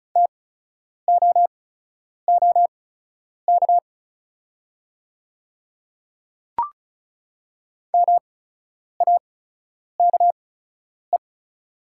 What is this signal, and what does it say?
Morse code, heard as a keyed tone, about 36 wpm, spaced as for 10 wpm, 700 Hz. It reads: TOOK MAKE
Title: 8x Character Spacing - Top 500 Words - 35wpm